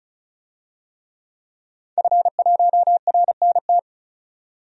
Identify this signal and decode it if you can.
Morse code, heard as a keyed tone, about 36 wpm, 700 Hz. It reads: F1RNT
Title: DXCC-100 Call Signs - 5 Characters - 35wpm